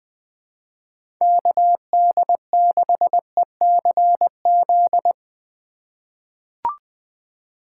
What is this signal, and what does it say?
Morse code, heard as a keyed tone, about 20 wpm, 700 Hz. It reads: KD6ECZ
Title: USA Call Signs - 6 Characters - 4x Character Spacing - 20wpm